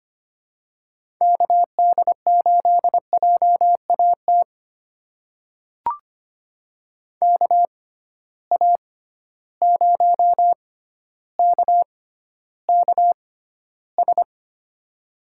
Morse code, heard as a keyed tone, about 25 wpm, spaced as for 13 wpm, 700 Hz. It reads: KD8JAT KA0KKS
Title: USA Call Signs - 6 Characters - 6x Character Spacing - 25wpm